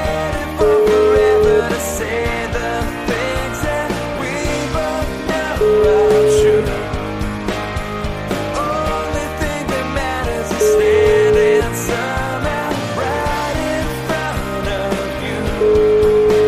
Ringback tone (Europe, 425 Hz), with noise.